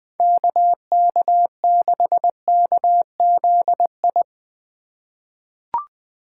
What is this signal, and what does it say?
Morse code, heard as a keyed tone, about 20 wpm, 700 Hz. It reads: KK6KZI
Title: USA Call Signs - 6 Characters - 4x Character Spacing - 20wpm